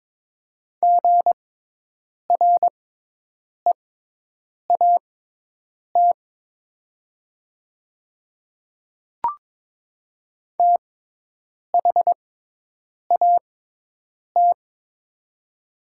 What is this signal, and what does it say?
Morse code, heard as a keyed tone, about 21 wpm, spaced as for 8 wpm, 700 Hz. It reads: GREAT THAT
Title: 6x Character Spacing - Top 300 Words - 22wpm